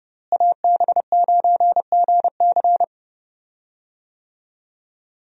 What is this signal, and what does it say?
Morse code, heard as a keyed tone, about 30 wpm, 700 Hz. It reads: AB9GC
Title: USA Call Signs - 5 Characters - Repeated 2x - 30wpm